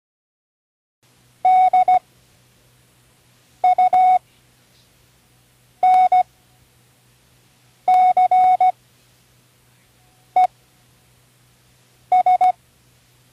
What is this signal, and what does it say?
Morse code, heard as a keyed tone, about 17 wpm, spaced as for 5 wpm, 740 Hz. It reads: DUNCES
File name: CW Learning Programs / Learning Morse Code (from Tune in the World) / 08 words